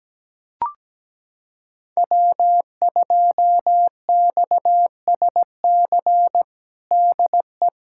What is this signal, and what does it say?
Morse code, heard as a keyed tone, about 17 wpm, 700 Hz. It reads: W2XSC DE
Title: USA Callsign DE Callsign - 5 Characters - 17wpm